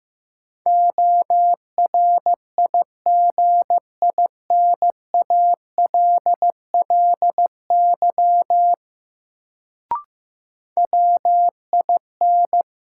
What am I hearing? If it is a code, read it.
Morse code, heard as a keyed tone, about 15 wpm, 700 Hz. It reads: ORIGINALLY WIN
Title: Sentences from Top 1500 Words - New Word Drill - 15wpm